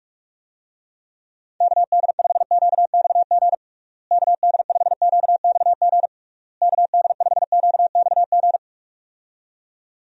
Morse code, heard as a keyed tone, about 46 wpm, 700 Hz. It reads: KD5QXG KD5QXG KD5QXG
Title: USA Call Signs - 6 Characters - Repeated 3x - 45wpm